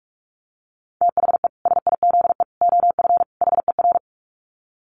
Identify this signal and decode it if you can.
Morse code, heard as a keyed tone, about 44 wpm, 700 Hz. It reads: THE SIZE OF HER
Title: Dit Challenge 45wpm